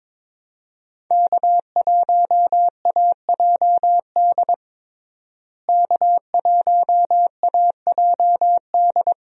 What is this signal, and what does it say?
Morse code, heard as a keyed tone, about 22 wpm, 700 Hz. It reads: K1AJD K1AJD
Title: USA Call Signs - 5 Characters - Repeated 2x - 22wpm